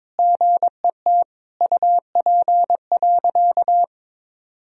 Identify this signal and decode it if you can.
Morse code, heard as a keyed tone, about 22 wpm, 700 Hz. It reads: GET UP.
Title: Sentences from Top 500 Words and Abbreviations 22wpm